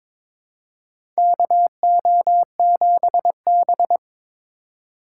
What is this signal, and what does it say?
Morse code, heard as a keyed tone, about 22 wpm, 700 Hz. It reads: KO7B
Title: USA Call Signs - 3x Character Spacing - 22wpm